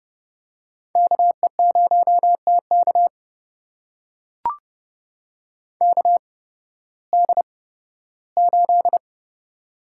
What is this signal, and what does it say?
Morse code, heard as a keyed tone, about 30 wpm, spaced as for 16 wpm, 700 Hz. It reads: KE0TK KD8
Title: USA Call Signs - 8x Character Spacing - 30wpm